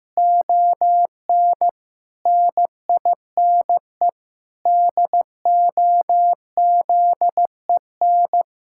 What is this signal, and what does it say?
Morse code, heard as a keyed tone, about 16 wpm, 700 Hz. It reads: ON NINE DOZEN
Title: Pangram Challenge v2 - 15wpm